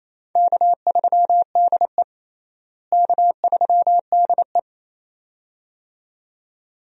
Morse code, heard as a keyed tone, about 28 wpm, 700 Hz. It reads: K3DE K3DE
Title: USA Call Signs - 4 Characters - Repeated 2x - 28wpm